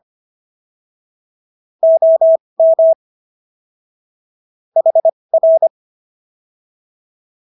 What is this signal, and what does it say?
Morse code, heard as a keyed tone, about 25 wpm, spaced as for 20 wpm, 650 Hz. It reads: OM HR